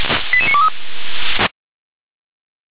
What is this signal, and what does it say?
Five-tone selective call (ZVEI), heard as quick runs of five tones, tones of about 70 ms. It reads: 88923